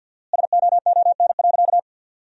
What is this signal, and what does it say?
Morse code, heard as a keyed tone, about 50 wpm, 700 Hz. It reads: SOON.
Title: 4x Word-Spacing - Sentences from Top 400 Words - 50wpm